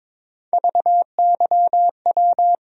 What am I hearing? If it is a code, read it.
Morse code, heard as a keyed tone, about 22 wpm, 700 Hz. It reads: VYW
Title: USA Call Sign Postfixes - 3 Letters 22wpm